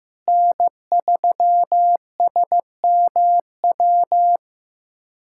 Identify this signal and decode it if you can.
Morse code, heard as a keyed tone, about 15 wpm, 700 Hz. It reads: N3SMW
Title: USA Callsign DE Callsign - 5 Characters - 15wpm